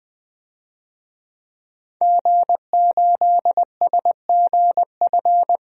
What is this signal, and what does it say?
Morse code, heard as a keyed tone, about 20 wpm, 700 Hz. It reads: G8SGF
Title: DXCC-100 Call Signs - 5 Characters - 20wpm